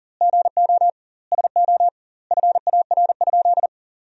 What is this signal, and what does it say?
Morse code, heard as a keyed tone, about 37 wpm, 700 Hz. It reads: GO SO FAR?
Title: 2x Word-Spacing - Sentences from Top 300 Words - 40wpm